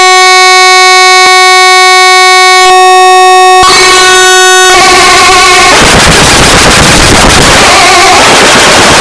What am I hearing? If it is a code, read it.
Dial-up modem handshake.